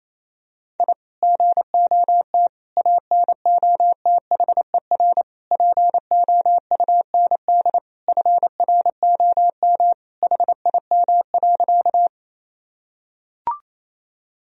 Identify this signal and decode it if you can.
Morse code, heard as a keyed tone, about 28 wpm, 700 Hz. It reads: I GOT ANOTHER POUND FROM HIM.
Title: Sentences from Top 1000 Words - Review Part 2 - 28wpm